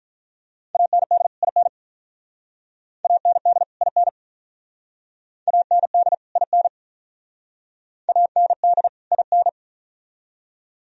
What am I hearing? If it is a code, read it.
Morse code, heard as a keyed tone, about 44 wpm, 700 Hz. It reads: AND IN AND IN AND IN AND IN